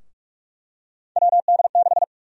Morse code, heard as a keyed tone, about 46 wpm, 700 Hz. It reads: WD6
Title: USA Call Sign Prefixes - 45wpm